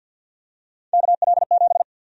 Morse code, heard as a keyed tone, about 52 wpm, 700 Hz. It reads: KL7
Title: USA Call Sign Prefixes - 50wpm